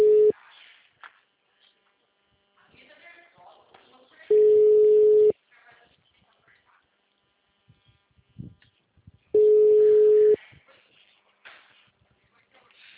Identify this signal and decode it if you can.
Ringback tone (Europe, 425 Hz), with speech.
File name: out- -2001- - - .19541